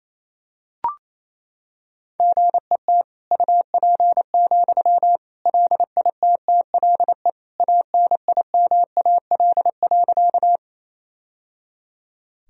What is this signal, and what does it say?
Morse code, heard as a keyed tone, about 28 wpm, 700 Hz. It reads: GET UP, LITTLE ANIMAL.
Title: Sentences from Top 1000 Words - Review Part 1 - 28wpm